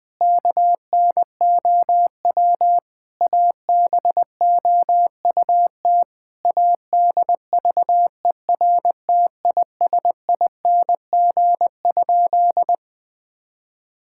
Morse code, heard as a keyed tone, about 21 wpm, 700 Hz. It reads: KNOW ABOUT ADVERTISING?
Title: Sentences from Top 1000 Words - Review Part 6 - 20wpm